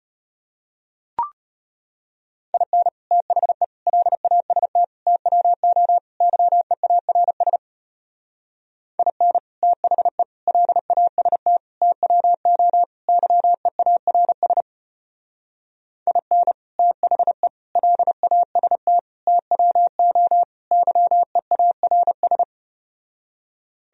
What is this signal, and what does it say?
Morse code, heard as a keyed tone, about 33 wpm, 700 Hz. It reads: IN THE LAST TWO YEARS IN THE LAST TWO YEARS IN THE LAST TWO YEARS